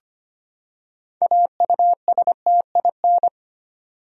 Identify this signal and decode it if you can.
Morse code, heard as a keyed tone, about 25 wpm, 700 Hz. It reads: AUSTIN
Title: USA Common Names - 25wpm